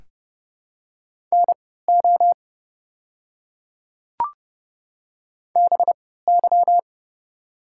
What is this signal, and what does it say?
Morse code, heard as a keyed tone, about 30 wpm, spaced as for 17 wpm, 700 Hz. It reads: NO BY
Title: 3x Character Spacing - Top 100 Words - 30wpm